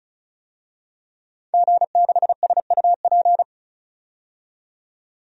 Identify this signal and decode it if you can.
Morse code, heard as a keyed tone, about 36 wpm, 700 Hz. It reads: G6SUP
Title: DXCC-100 Call Signs - 5 Characters - 35wpm